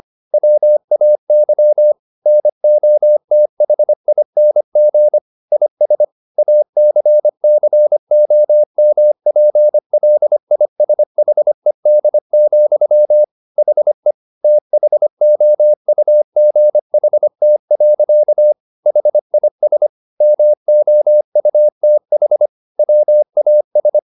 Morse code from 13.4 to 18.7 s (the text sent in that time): HE THOUGHT.